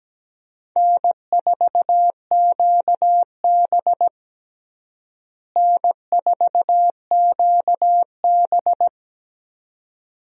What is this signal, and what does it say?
Morse code, heard as a keyed tone, about 17 wpm, 700 Hz. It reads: N4QB N4QB
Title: USA Call Signs - 4 Characters - Repeated 2x - 17wpm